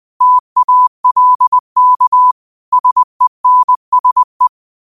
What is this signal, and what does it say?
Morse code, heard as a keyed tone, about 20 wpm, 1000 Hz. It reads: TALK SENSE
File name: qotdfiles / 20wpm